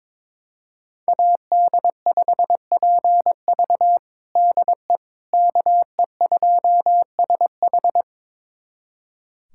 Morse code, heard as a keyed tone, about 21 wpm, 700 Hz. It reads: AD5PV DE KE2SH
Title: USA Callsign DE Callsign - 5 Characters - 22wpm